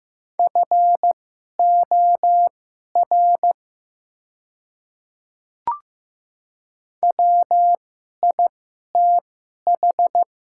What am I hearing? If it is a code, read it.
Morse code, heard as a keyed tone, about 15 wpm, spaced as for 11 wpm, 700 Hz. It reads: FOR WITH
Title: 2x Character Spacing - Top 100 Words - 15wpm